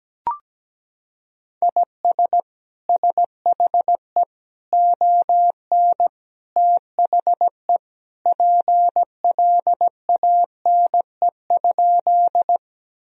Morse code, heard as a keyed tone, about 17 wpm, 700 Hz. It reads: IS SHE ON THE PLANE?
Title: Sentences from Top 1000 Words - Review Part 3 - 17wpm